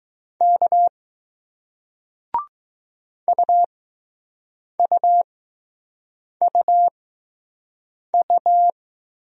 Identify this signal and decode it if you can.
Morse code, heard as a keyed tone, about 20 wpm, spaced as for 6 wpm, 700 Hz. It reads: K UUUU